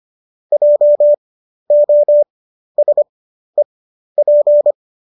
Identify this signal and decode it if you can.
Morse code, heard as a keyed tone, about 25 wpm, spaced as for 12 wpm, 590 Hz. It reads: JOSEP
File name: qso009 12